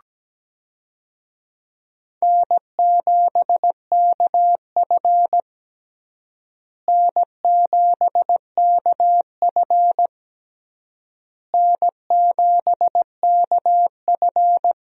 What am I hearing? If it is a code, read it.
Morse code, heard as a keyed tone, about 17 wpm, 700 Hz. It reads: N7KF N7KF N7KF
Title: USA Call Signs - 4 Characters - Repeated 3x - 17wpm